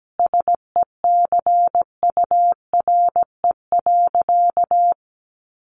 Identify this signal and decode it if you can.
Morse code, heard as a keyed tone, about 17 wpm, 700 Hz. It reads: SECURE.
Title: Sentences from Top 1200 Words 17wpm